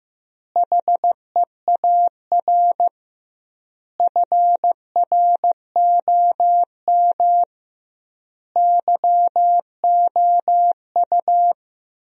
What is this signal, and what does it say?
Morse code, heard as a keyed tone, about 15 wpm, 700 Hz. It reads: HEAR FROM YOU
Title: 2x Word-Spacing - Sentences from Top 400 Words - 15wpm